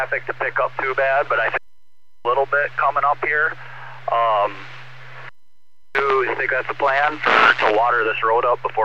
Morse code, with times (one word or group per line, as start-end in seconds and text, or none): none